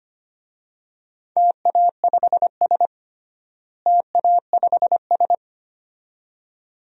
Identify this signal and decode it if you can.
Morse code, heard as a keyed tone, about 25 wpm, 700 Hz. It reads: TA5S TA5S
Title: DXCC-100 Call Signs - 4 Characters - Repeated 2x - 25wpm